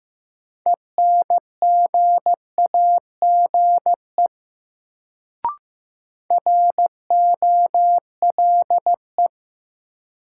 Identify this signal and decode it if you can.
Morse code, heard as a keyed tone, about 15 wpm, 700 Hz. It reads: ENGAGE ROLE
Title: Sentences from Top 1500 Words - New Word Drill Alt - 15wpm